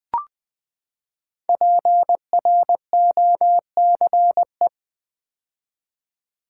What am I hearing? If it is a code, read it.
Morse code, heard as a keyed tone, about 20 wpm, 700 Hz. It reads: PROCE
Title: Build A Word - 5 Letters At A Time - 10-Letter Words - 20wpm